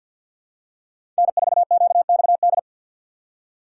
Morse code, heard as a keyed tone, about 53 wpm, 700 Hz. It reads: N4QXD
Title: USA Call Signs - 5 Characters - 4x Character Spacing - 50wpm